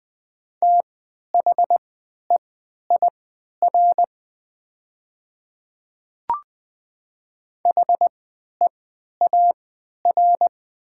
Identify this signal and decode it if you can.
Morse code, heard as a keyed tone, about 20 wpm, spaced as for 11 wpm, 700 Hz. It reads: THEIR HEAR
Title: 3x Character Spacing - Top 300 Words - 20wpm